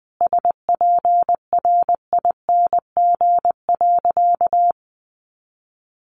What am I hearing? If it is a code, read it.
Morse code, heard as a keyed tone, about 20 wpm, 700 Hz. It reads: SPRING.
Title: Sentences from Top 1200 Words 20wpm